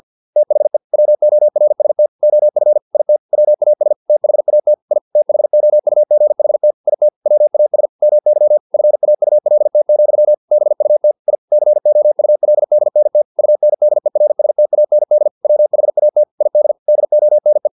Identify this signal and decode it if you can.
Morse code, heard as a keyed tone, about 50 wpm, 600 Hz. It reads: THE WORST OF IT WAS THAT I THOUGHT IT WAS MY FAULT, BUT I COULDNT UNDERSTAND WHAT ID DONE
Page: Never 49wpm